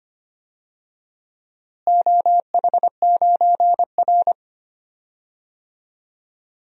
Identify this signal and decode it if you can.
Morse code, heard as a keyed tone, about 25 wpm, 700 Hz. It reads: OH9R
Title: DXCC-100 Call Signs - 4 Characters - Repeated 3x - 25wpm